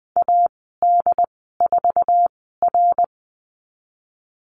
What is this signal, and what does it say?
Morse code, heard as a keyed tone, about 20 wpm, spaced as for 14 wpm, 700 Hz. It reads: AD4R
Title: 082 Lesson 14g - Number 2 - Call Signs 20wpm